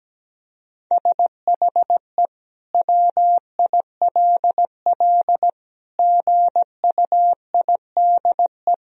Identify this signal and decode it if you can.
Morse code, heard as a keyed tone, about 17 wpm, 700 Hz. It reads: SHE WILL GUIDE
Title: Sentences from Top 1000 Words - Review Part 3 - 17wpm